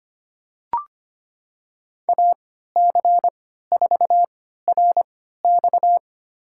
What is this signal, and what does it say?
Morse code, heard as a keyed tone, about 25 wpm, spaced as for 14 wpm, 700 Hz. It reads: AC4RX
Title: USA Call Signs - 5 Characters - 3x Character Spacing - 25wpm